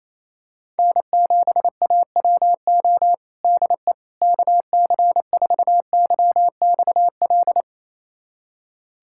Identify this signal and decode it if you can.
Morse code, heard as a keyed tone, about 28 wpm, 700 Hz. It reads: N7AWO DE KC4YXL
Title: USA Callsign DE Callsign 28wpm